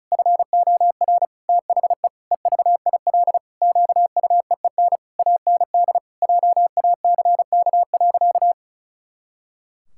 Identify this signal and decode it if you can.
Morse code, heard as a keyed tone, about 34 wpm, 700 Hz. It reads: FOR THE EVIL QUEEN AND JACK.
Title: Pangram Challenge v2 - 35wpm